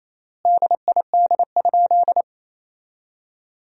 Morse code, heard as a keyed tone, about 28 wpm, 700 Hz. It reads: DID?
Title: 3x Word-Spacing - Sentences from Top 100 Words - 28wpm